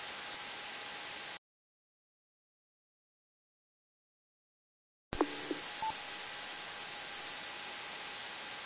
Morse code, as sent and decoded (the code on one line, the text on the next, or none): none